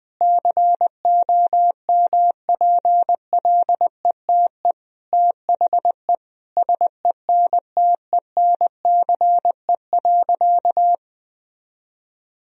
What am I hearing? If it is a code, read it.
Morse code, heard as a keyed tone, about 20 wpm, 700 Hz. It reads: COMPLETE THE SENTENCE.
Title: Sentences from Top 1000 Words - Review Part 1 - 20wpm